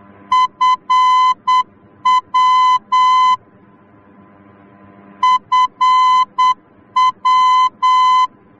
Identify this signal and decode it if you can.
Morse code, heard as a keyed tone, about 8 wpm, 1020 Hz. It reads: FW FW